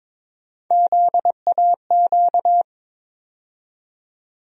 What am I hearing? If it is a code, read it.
Morse code, heard as a keyed tone, about 21 wpm, 700 Hz. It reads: ZAQ
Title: USA Call Sign Postfixes - 3 Letters 22wpm